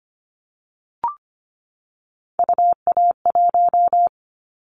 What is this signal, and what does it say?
Morse code, heard as a keyed tone, about 25 wpm, 700 Hz. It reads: UA1
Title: Callsign Prefixes for Top 25 DXCC - 25wpm